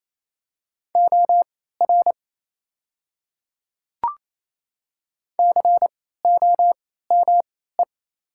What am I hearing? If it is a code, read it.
Morse code, heard as a keyed tone, about 28 wpm, spaced as for 16 wpm, 700 Hz. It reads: OR COME
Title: 3x Character Spacing - Top 100 Words - 28wpm